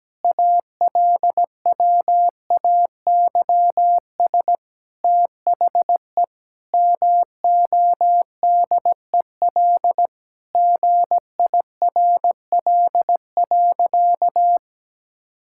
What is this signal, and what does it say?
Morse code, heard as a keyed tone, about 17 wpm, 700 Hz. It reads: ALWAYS THE MODEL GIRL.
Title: Sentences from Top 1000 Words - Review Part 5 - 17wpm